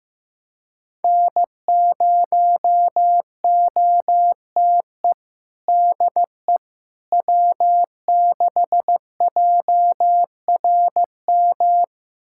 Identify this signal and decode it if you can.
Morse code, heard as a keyed tone, about 15 wpm, 700 Hz. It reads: N0OTE DE W6JRM
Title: USA Callsign DE Callsign - 5 Characters - 15wpm